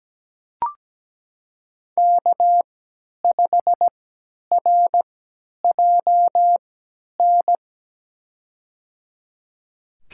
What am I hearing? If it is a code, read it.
Morse code, heard as a keyed tone, about 17 wpm, spaced as for 10 wpm, 700 Hz. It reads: K5RJN
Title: USA Call Signs - 5 Characters - 3x Character Spacing - 17wpm